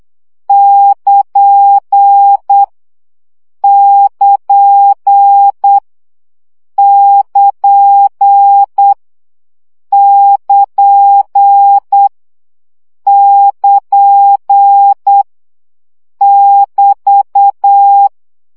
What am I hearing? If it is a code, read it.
Morse code, heard as a keyed tone, about 8 wpm, 800 Hz. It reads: <KN> <KN> <KN> <KN> <KN> <BT>
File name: CW gyakorlas nyito 7 baud